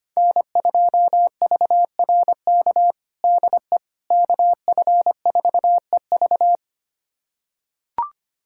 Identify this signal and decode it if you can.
Morse code, heard as a keyed tone, about 25 wpm, 700 Hz. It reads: N2VRK DE KF4EV
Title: USA Callsign DE Callsign - 5 Characters - 25wpm